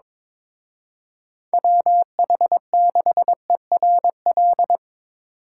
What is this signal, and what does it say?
Morse code, heard as a keyed tone, about 24 wpm, 700 Hz. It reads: WH6ERL
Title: Speed Racing - USA 2x3 Call Signs 22wpm